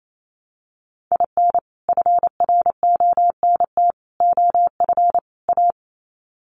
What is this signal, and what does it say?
Morse code, heard as a keyed tone, about 28 wpm, 700 Hz. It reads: IN FRONT OF A
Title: Sets of 4 Words 28wpm